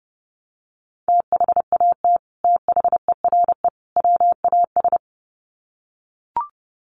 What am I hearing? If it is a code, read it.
Morse code, heard as a keyed tone, about 30 wpm, 700 Hz. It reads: THAT THERE WAS
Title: Sets of 3 Words 30wpm